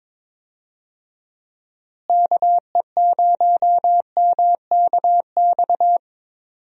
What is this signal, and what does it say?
Morse code, heard as a keyed tone, about 22 wpm, 700 Hz. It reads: KE0MKX
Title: USA Call Signs - 6 Characters - Repeated 3x - 22wpm